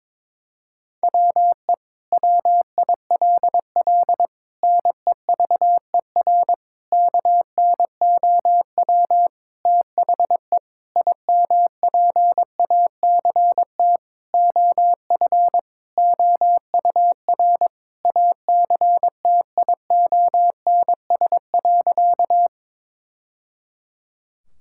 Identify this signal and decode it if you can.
Morse code, heard as a keyed tone, about 22 wpm, 700 Hz. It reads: WE WILL NEVER KNOW THE IMPACT OF OUR ACTIONS.
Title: Sentences from Top 1000 Words - Review Part 6 - 22wpm